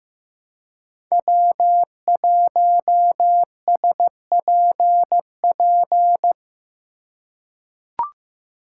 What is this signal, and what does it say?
Morse code, heard as a keyed tone, about 15 wpm, 700 Hz. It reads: W1SPP
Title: USA Call Signs - 5 Characters - 3x Character Spacing - 15wpm